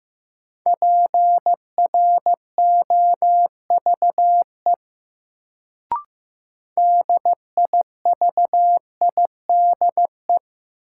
Morse code, heard as a keyed tone, about 15 wpm, 700 Hz. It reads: PROVE DIVIDE
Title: Sentences from Top 1600 Words - New Word Drill Alt - 15wpm